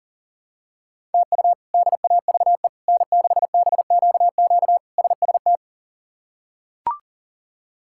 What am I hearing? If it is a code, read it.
Morse code, heard as a keyed tone, about 40 wpm, 700 Hz. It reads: TU DAVE N6BQQ SST
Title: K1USN SST QSO Practice - 40wpm